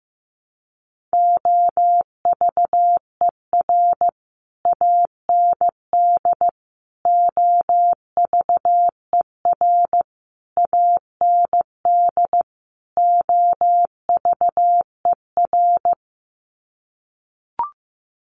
Morse code, heard as a keyed tone, about 15 wpm, 700 Hz. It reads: OVER AND OVER AND OVER
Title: Sets of 5 Words Encore 15wpm